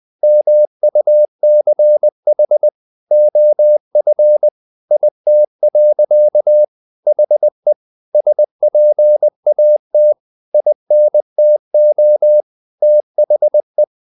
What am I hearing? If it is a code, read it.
Morse code, heard as a keyed tone, about 20 wpm, 600 Hz. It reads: MUCH OF IT. HE SPAT INTO THE
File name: the old man and the sea part 2 20wpm